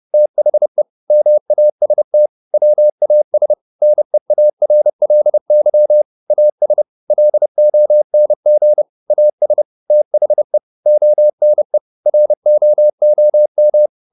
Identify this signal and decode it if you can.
Morse code, heard as a keyed tone, about 30 wpm, 600 Hz. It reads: THE MAST WAS NEARLY AS LONG AS THE ONE ROOM
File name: the old man and the sea / the old man and the sea 1 30wpm